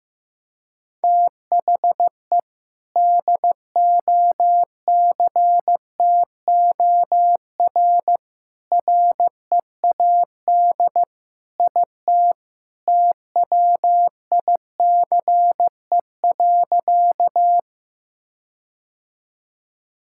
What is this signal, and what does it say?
Morse code, heard as a keyed tone, about 15 wpm, 700 Hz. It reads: THE DOCTOR READ IT TWICE.
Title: Sentences from Top 1600 Words 15wpm